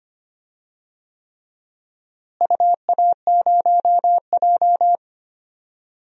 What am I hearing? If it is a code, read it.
Morse code, heard as a keyed tone, about 25 wpm, 700 Hz. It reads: UA0J
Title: DXCC-100 Call Signs - 4 Characters - Repeated 3x - 25wpm